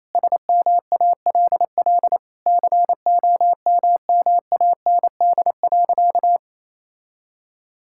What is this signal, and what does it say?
Morse code, heard as a keyed tone, about 28 wpm, 700 Hz. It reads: SMALL COMMAND.
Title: Sentences from Top 1600 Words 28wpm